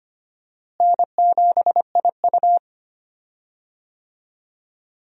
Morse code, heard as a keyed tone, about 25 wpm, 700 Hz. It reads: N7IU
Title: USA Call Signs - 4 Characters - Repeated 3x - 25wpm